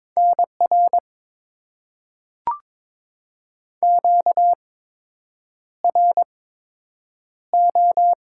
Morse code, heard as a keyed tone, about 22 wpm, spaced as for 8 wpm, 700 Hz. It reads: NR QRO
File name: Top 90 QSO Elements - 8x Character Spacing - 22wpm